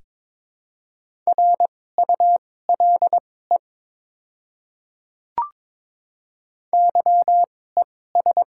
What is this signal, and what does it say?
Morse code, heard as a keyed tone, about 22 wpm, spaced as for 16 wpm, 690 Hz. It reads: RULE YES
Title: 2x Character Spacing - Top 500 Words - 22wpm